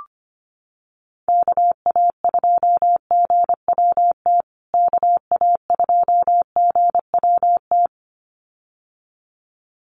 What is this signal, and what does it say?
Morse code, heard as a keyed tone, about 25 wpm, 700 Hz. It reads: KA2GWT KA2GWT
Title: USA Call Signs Repeated 2x 25wpm